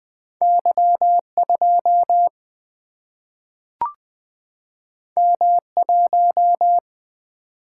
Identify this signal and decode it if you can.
Morse code, heard as a keyed tone, about 20 wpm, 700 Hz. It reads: Y 2 M 1
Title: Single-Letter Single-Number 20wpm